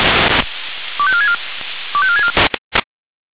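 Five-tone selective call (ZVEI), heard as quick runs of five tones, tones of about 70 ms. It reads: 25464 25464
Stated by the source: ZVEI five-tone selective call received via SDR.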